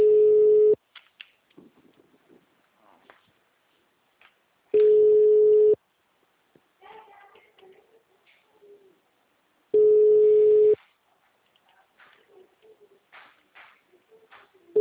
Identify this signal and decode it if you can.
Ringback tone (Europe, 425 Hz), with speech.